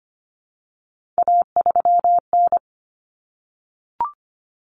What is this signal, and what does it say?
Morse code, heard as a keyed tone, about 25 wpm, 700 Hz. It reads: A3N
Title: Common Cut Numbers - 3-Digit 25wpm